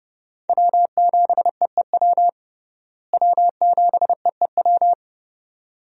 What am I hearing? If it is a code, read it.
Morse code, heard as a keyed tone, about 30 wpm, 700 Hz. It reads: W7EEW W7EEW
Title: USA Call Signs - 5 Characters - Repeated 2x - 30wpm